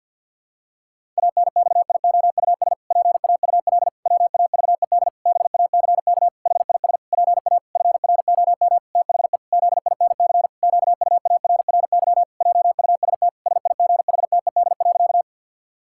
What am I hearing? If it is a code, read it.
Morse code, heard as a keyed tone, about 49 wpm, 700 Hz. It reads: ANXIOUS PAUL WAVED BACK HIS PA FROM THE ZINC QUARRY JUST SIGHTED.
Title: Pangram Challenge v2 - 50wpm